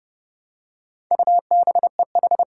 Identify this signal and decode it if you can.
Morse code, heard as a keyed tone, about 30 wpm, 700 Hz. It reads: UBEH
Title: Four Challenging Letter-Number Combinations 30wpm